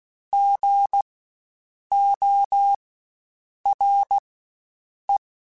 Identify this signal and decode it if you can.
Morse code, heard as a keyed tone, about 16 wpm, spaced as for 7 wpm, 800 Hz. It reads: GORE
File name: Spaced Alphas / Spaced Alphas mp3s / Gs sp @15